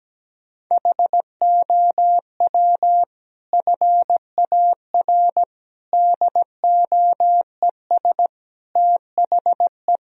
Morse code, heard as a keyed tone, about 17 wpm, 700 Hz. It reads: HOW FAR DOES THE
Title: Sentences from Top 1000 Words - Review Part 4 - 17wpm